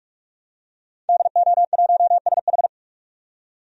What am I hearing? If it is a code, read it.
Morse code, heard as a keyed tone, about 46 wpm, 700 Hz. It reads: DO1SH
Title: DXCC-100 Call Signs - 5 Characters - 45wpm